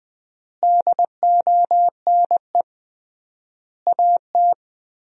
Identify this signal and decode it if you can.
Morse code, heard as a keyed tone, about 21 wpm, 700 Hz. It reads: DONE AT
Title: 3x Word-Spacing - Sentences from Top 500 Words - 20wpm